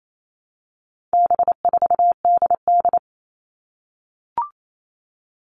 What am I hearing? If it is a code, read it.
Morse code, heard as a keyed tone, about 28 wpm, 700 Hz. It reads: B4DD
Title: All Cut Numbers - 4-Digit 28wpm